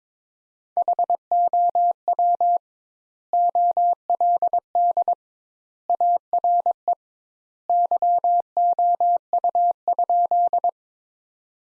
Morse code, heard as a keyed tone, about 22 wpm, 700 Hz. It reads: HOW OLD ARE YOU?
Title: Sentences Top 200 Words - 22wpm